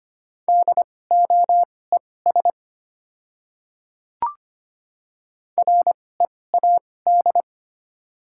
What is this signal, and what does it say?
Morse code, heard as a keyed tone, about 25 wpm, spaced as for 18 wpm, 690 Hz. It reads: DOES READ
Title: 2x Character Spacing - Top 200 Words - 25wpm